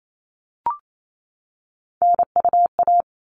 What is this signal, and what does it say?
Morse code, heard as a keyed tone, about 28 wpm, 700 Hz. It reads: NUA
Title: All Cut Numbers - 3-Digit 28wpm